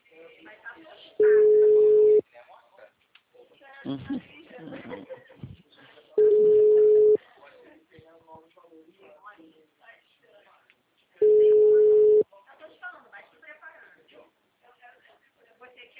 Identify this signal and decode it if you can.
Ringback tone (Europe, 425 Hz), with speech.
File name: out- -2033- - - .23523